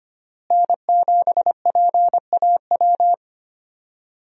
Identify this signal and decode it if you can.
Morse code, heard as a keyed tone, about 25 wpm, 700 Hz. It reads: N7PAW
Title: USA Call Signs - 5 Characters - Repeated 3x - 25wpm